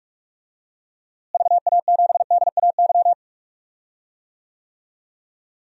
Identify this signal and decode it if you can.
Morse code, heard as a keyed tone, about 43 wpm, 700 Hz. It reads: VA7DAY
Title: DXCC-100 Call Signs - 6 Characters - Repeated 3x - 45wpm